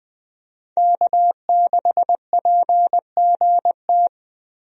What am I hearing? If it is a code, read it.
Morse code, heard as a keyed tone, about 20 wpm, 700 Hz. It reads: K6PGT
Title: USA Call Signs - 5 Characters - Repeated 3x - 20wpm